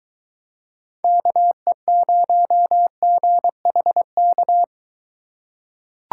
Morse code, heard as a keyed tone, about 23 wpm, 700 Hz. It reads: KE0GHK